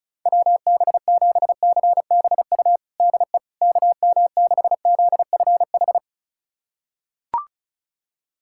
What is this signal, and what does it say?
Morse code, heard as a keyed tone, about 34 wpm, 700 Hz. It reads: WB7CBU DE KM6ZFH
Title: USA Callsign DE Callsign - 6 Characters - 35wpm